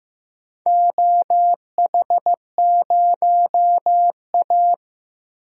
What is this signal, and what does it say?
Morse code, heard as a keyed tone, about 15 wpm, 700 Hz. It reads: OH0A
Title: DXCC-100 Call Signs - 4 Characters - 15wpm